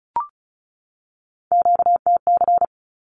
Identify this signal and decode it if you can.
Morse code, heard as a keyed tone, about 34 wpm, 700 Hz. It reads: QTC
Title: Q-Codes 35wpm